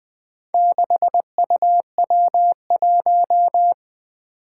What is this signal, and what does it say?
Morse code, heard as a keyed tone, about 20 wpm, 700 Hz. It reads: 6UW1
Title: Four Challenging Letter-Number Combinations 20wpm